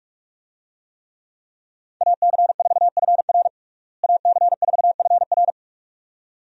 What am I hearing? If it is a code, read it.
Morse code, heard as a keyed tone, about 46 wpm, 700 Hz. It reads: AC4FR AC4FR
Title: USA Call Signs - 5 Characters - Repeated 2x - 45wpm